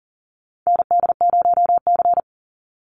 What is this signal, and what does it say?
Morse code, heard as a keyed tone, about 40 wpm, 700 Hz. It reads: ND0C
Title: NA SOTA Chaser Callsigns 40wpm